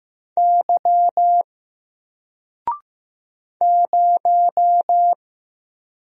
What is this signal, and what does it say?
Morse code, heard as a keyed tone, about 15 wpm, 700 Hz. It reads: Y 0
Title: Single Letter-Number Alt 15wpm